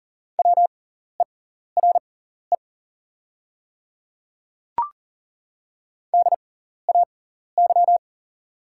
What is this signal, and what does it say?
Morse code, heard as a keyed tone, about 40 wpm, spaced as for 14 wpm, 700 Hz. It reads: WERE DAY
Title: 6x Character Spacing - Top 100 Words - 40wpm